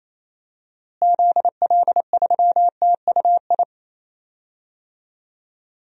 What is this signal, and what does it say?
Morse code, heard as a keyed tone, about 28 wpm, 700 Hz. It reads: ZL3TUI
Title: DXCC-100 Call Signs - 6 Characters - Repeated 2x - 28wpm